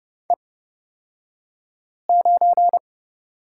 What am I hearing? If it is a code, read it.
Morse code, heard as a keyed tone, about 29 wpm, spaced as for 5 wpm, 700 Hz. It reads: E9
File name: Lesson 36d - BT - Two Characters 30wpm